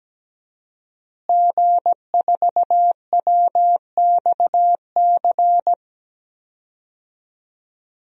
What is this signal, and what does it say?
Morse code, heard as a keyed tone, about 17 wpm, 700 Hz. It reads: G4WXC
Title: DXCC-100 Call Signs - 5 Characters - Repeated 2x - 17wpm